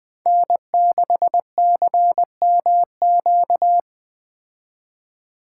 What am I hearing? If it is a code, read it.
Morse code, heard as a keyed tone, about 20 wpm, 700 Hz. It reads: N6CMQ
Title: Speed Racing - USA 1x3 Call Signs 17wpm